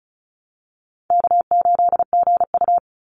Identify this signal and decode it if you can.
Morse code, heard as a keyed tone, about 34 wpm, 700 Hz. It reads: K8GU
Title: Easy Call Signs Repeated 3x - 35wpm